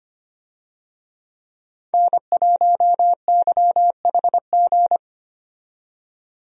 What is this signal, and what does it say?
Morse code, heard as a keyed tone, about 25 wpm, 700 Hz. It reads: N1YHG
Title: USA Call Signs - 5 Characters - 25wpm